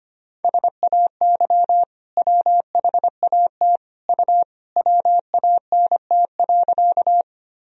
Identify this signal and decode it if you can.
Morse code, heard as a keyed tone, about 25 wpm, 700 Hz. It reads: SAY WHAT U WANT.
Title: Sentences from Top 200 Words and Abbreviations 25wpm